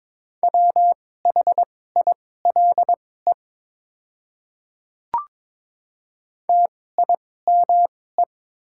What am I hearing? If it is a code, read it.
Morse code, heard as a keyed tone, about 22 wpm, spaced as for 16 wpm, 700 Hz. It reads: WHILE TIME
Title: 2x Character Spacing - Top 300 Words - 22wpm